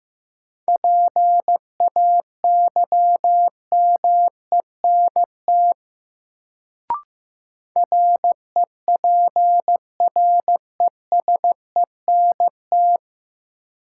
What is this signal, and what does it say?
Morse code, heard as a keyed tone, about 15 wpm, 700 Hz. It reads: PAYMENT REPRESENT
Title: Sentences from Top 1200 Words - New Word Drill - 15wpm